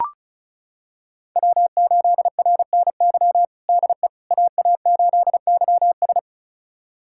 Morse code, heard as a keyed tone, about 34 wpm, 700 Hz. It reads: W8RNY DE AA8YS
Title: USA Callsign DE Callsign 35wpm